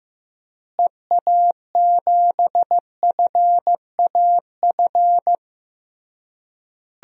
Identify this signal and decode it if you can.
Morse code, heard as a keyed tone, about 15 wpm, 700 Hz. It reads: EA7FAF
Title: DXCC-100 Call Signs - 6 Characters - Repeated 3x - 15wpm